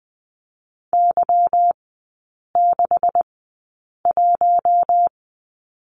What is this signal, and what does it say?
Morse code, heard as a keyed tone, about 19 wpm, spaced as for 8 wpm, 700 Hz. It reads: Y61
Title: 166 Lesson 27e - Number 7 - Three Characters 20wpm